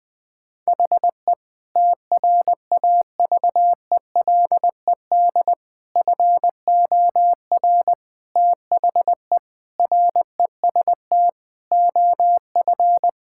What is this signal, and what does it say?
Morse code, heard as a keyed tone, about 20 wpm, 700 Hz. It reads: HE TRAVELED FOR THE REST OF
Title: Sentences with 5-Word Phrases - 20wpm